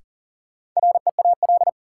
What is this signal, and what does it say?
Morse code, heard as a keyed tone, about 40 wpm, 700 Hz. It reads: REAL
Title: Sentences from Top 300 Words - New Word Drill - 40wpm